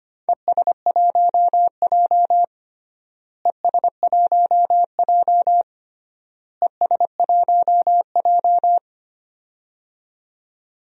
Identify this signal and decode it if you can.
Morse code, heard as a keyed tone, about 25 wpm, 700 Hz. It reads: ES1J ES1J ES1J
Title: DXCC-100 Call Signs - 4 Characters - Repeated 3x - 25wpm